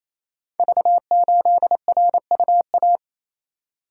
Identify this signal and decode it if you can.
Morse code, heard as a keyed tone, about 27 wpm, 700 Hz. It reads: V8RUA
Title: Code Groups - 5 Letter:Numbers - 28wpm